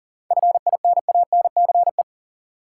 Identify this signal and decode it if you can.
Morse code, heard as a keyed tone, about 40 wpm, 700 Hz. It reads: FINANCE
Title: Seven Letter Words - 40wpm